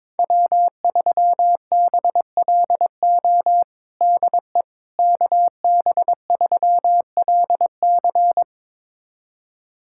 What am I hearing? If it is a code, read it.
Morse code, heard as a keyed tone, about 22 wpm, 700 Hz. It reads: W3BLO DE KB3LC
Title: USA Callsign DE Callsign - 5 Characters - 22wpm